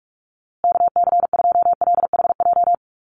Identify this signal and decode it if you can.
Morse code, heard as a keyed tone, about 44 wpm, 700 Hz. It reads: KC2LHJ
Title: USA Call Signs Repeated 2x 45wpm